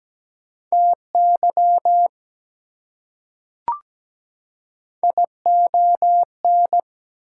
Morse code, heard as a keyed tone, about 17 wpm, 700 Hz. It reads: TY ION
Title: Prefixes and Suffixes 17wpm